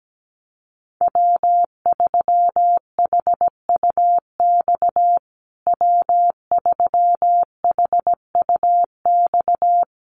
Morse code, heard as a keyed tone, about 17 wpm, 700 Hz. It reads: W3HUX W3HUX
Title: USA Call Signs Repeated 2x 17wpm